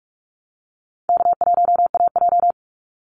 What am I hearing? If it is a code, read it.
Morse code, heard as a keyed tone, about 44 wpm, 700 Hz. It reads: K1AJ
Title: Easy Call Signs Repeated 3x - 45wpm